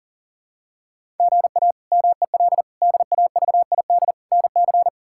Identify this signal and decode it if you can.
Morse code, heard as a keyed tone, about 40 wpm, 700 Hz. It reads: GA MEL DAVID NC
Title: K1USN SST QSO Practice - 40wpm